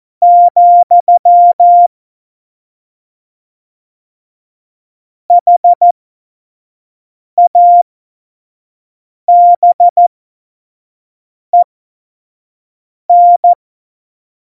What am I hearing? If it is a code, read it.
Morse code, heard as a keyed tone, about 14 wpm, spaced as for 5 wpm, 700 Hz. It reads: , HABEN